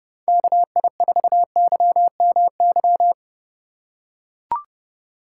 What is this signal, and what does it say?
Morse code, heard as a keyed tone, about 29 wpm, 700 Hz. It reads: KI4YMY